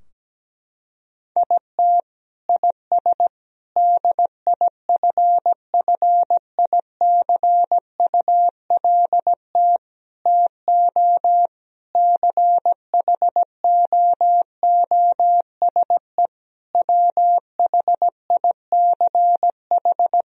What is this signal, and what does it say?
Morse code, heard as a keyed tone, about 17 wpm, 700 Hz. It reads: IT IS DIFFICULT TO CHOOSE WHICH
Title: Sentences from Top 1000 Words - Review Part 3 - 17wpm